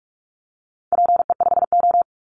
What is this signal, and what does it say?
Morse code, heard as a keyed tone, about 44 wpm, 700 Hz. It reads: PE5O
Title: Call Signs - 45wpm